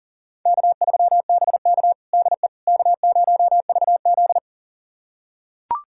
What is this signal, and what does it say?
Morse code, heard as a keyed tone, about 40 wpm, 700 Hz. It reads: K3BK DE K0VZ
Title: USA Callsign DE Callsign - 4 Characters - 40wpm